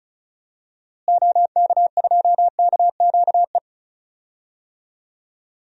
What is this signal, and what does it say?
Morse code, heard as a keyed tone, about 36 wpm, 700 Hz. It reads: OK2KQE
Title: DXCC-100 Call Signs - 6 Characters - Repeated 2x - 35wpm